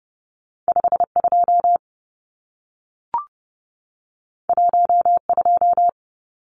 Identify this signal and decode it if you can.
Morse code, heard as a keyed tone, about 30 wpm, 700 Hz. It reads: 52 12
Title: Numbers - 2-Digit 30wpm